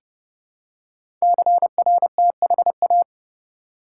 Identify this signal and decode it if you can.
Morse code, heard as a keyed tone, about 30 wpm, 700 Hz. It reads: CRTHA
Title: Code Groups - 5 Letters - 30wpm